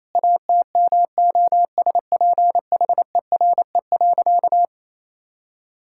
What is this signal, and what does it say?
Morse code, heard as a keyed tone, about 28 wpm, 700 Hz. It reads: ATMOSPHERE.
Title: Sentences from Top 1600 Words 28wpm